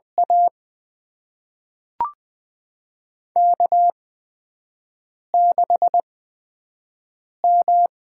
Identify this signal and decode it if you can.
Morse code, heard as a keyed tone, about 20 wpm, spaced as for 5 wpm, 700 Hz. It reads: A K6M
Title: USA Call Signs - 8x Character Spacing - 20wpm